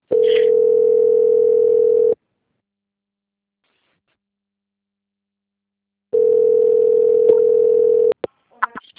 Ringback tone (North America), 440 + 480 Hz.
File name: out- - - - - .2710